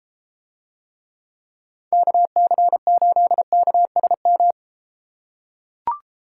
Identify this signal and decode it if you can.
Morse code, heard as a keyed tone, about 33 wpm, 700 Hz. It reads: KC8KSM